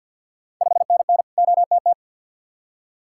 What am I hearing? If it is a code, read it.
Morse code, heard as a keyed tone, about 53 wpm, 700 Hz. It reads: 5NN OTT
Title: Ontario QSO Party Exchange 50wpm